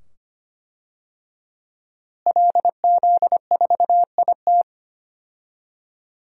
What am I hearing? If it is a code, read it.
Morse code, heard as a keyed tone, about 25 wpm, 700 Hz. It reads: LZ4IT
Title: DXCC-100 Call Signs - 5 Characters - Repeated 2x - 25wpm